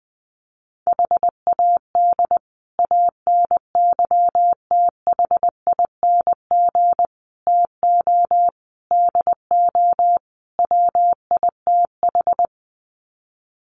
Morse code, heard as a keyed tone, about 20 wpm, 700 Hz. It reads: HAD ANYTHING TO DO WITH
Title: Sets of 5 Words Encore 20wpm